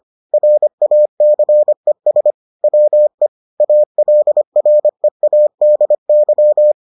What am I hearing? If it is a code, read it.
Morse code, heard as a keyed tone, about 25 wpm, 600 Hz. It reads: RACES WE ALREADY